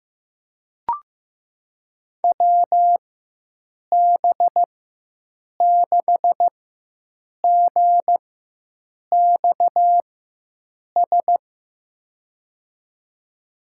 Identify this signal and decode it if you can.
Morse code, heard as a keyed tone, about 15 wpm, spaced as for 7 wpm, 700 Hz. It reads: WB6GXS
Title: USA Call Signs - 6 Characters - 4x Character Spacing - 15wpm